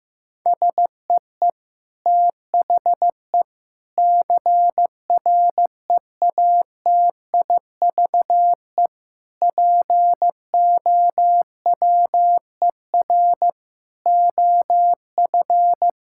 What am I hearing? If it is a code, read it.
Morse code, heard as a keyed tone, about 15 wpm, 700 Hz. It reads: SEE THE CREATIVE POWER OF
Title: Sentences from Top 1000 Words - Review Part 5 - 15wpm